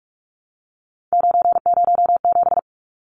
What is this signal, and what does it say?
Morse code, heard as a keyed tone, about 44 wpm, 700 Hz. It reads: 907
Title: Numbers - 3-Digit 45wpm